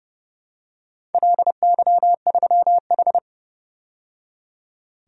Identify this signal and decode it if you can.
Morse code, heard as a keyed tone, about 30 wpm, 700 Hz. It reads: LY3H
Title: DXCC-100 Call Signs - 4 Characters - 30wpm